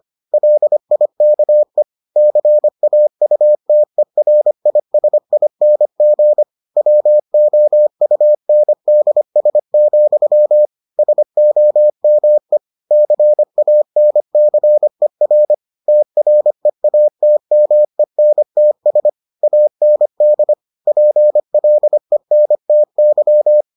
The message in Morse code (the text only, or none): LIKE CAUTERISING WOUNDS, SOME CANCER TREATMENTS AND PLENTY